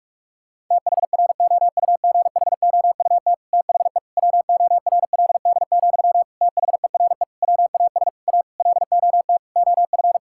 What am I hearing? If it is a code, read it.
Morse code, heard as a keyed tone, about 41 wpm, 700 Hz. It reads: THROUGHOUT THE WORLD, THERE WAS A LOT OF
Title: Sentences from Top 1000 Words - Review Part 5 - 45wpm